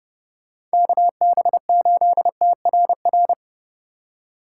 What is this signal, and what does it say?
Morse code, heard as a keyed tone, about 34 wpm, 700 Hz. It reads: KB8TRR